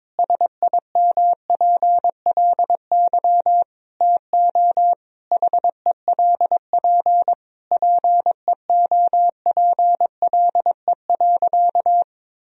Morse code, heard as a keyed tone, about 21 wpm, 700 Hz. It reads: SIMPLY TO HELP PEOPLE.
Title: Sentences from Top 1000 Words - Review Part 4 - 22wpm